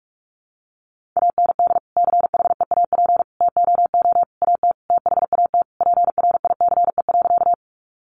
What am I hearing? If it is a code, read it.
Morse code, heard as a keyed tone, about 44 wpm, 700 Hz. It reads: AND CHEAP TOO AT THAT PRICE.
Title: Sentences from Top 1200 Words 45wpm